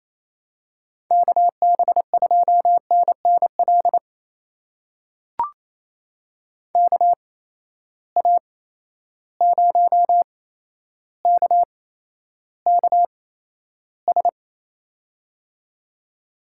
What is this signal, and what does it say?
Morse code, heard as a keyed tone, about 28 wpm, spaced as for 12 wpm, 700 Hz. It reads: KB2NNL KA0KKS
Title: USA Call Signs - 6 Characters - 8x Character Spacing - 28wpm